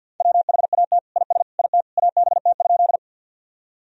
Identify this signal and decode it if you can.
Morse code, heard as a keyed tone, about 53 wpm, 700 Hz. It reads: WHAT IS IT ABT?
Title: Sentences from Top 100 Words and Abbreviations 50wpm